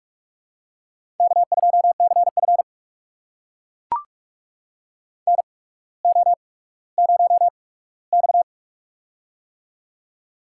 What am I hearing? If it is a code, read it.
Morse code, heard as a keyed tone, about 42 wpm, spaced as for 19 wpm, 700 Hz. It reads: K2CF NO0X
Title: USA Call Signs - 8x Character Spacing - 45wpm